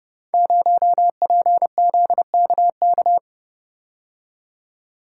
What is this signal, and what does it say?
Morse code, heard as a keyed tone, about 28 wpm, 700 Hz. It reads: 0PZKK
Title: Code Groups - 5 Letter:Numbers - 30wpm